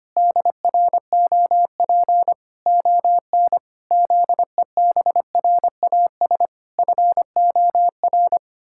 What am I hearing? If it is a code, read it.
Morse code, heard as a keyed tone, about 25 wpm, 700 Hz. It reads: DROP ON ZEBRAS FOR
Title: Pangram Challenge v2 - 25wpm